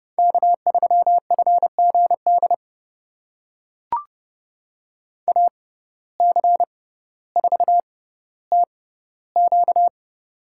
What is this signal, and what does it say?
Morse code, heard as a keyed tone, about 30 wpm, spaced as for 15 wpm, 700 Hz. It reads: K3FGD AC4TQ
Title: USA Call Signs - 5 Characters - 6x Character Spacing - 30wpm